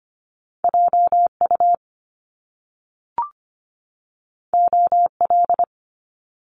Morse code, heard as a keyed tone, about 25 wpm, 700 Hz. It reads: JU OL
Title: Top Two Letter Combinations in Words 25wpm